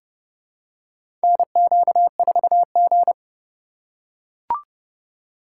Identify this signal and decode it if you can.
Morse code, heard as a keyed tone, about 30 wpm, 700 Hz. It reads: NQ4G
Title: USA Call Signs - 4x Character Spacing - 30wpm